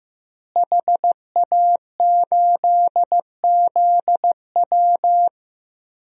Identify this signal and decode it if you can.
Morse code, heard as a keyed tone, about 15 wpm, 700 Hz. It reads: HA8ZW
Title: DXCC-100 Call Signs - 5 Characters - Repeated 2x - 15wpm